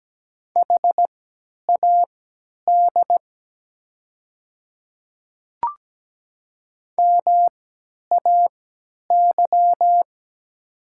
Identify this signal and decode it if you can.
Morse code, heard as a keyed tone, about 17 wpm, spaced as for 10 wpm, 700 Hz. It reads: HAD MAY
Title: 3x Character Spacing - Top 100 Words - 17wpm